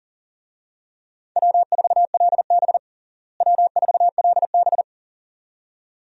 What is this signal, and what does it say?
Morse code, heard as a keyed tone, about 37 wpm, 700 Hz. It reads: W4LB W4LB
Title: USA Call Signs - 4 Characters - Repeated 2x - 40wpm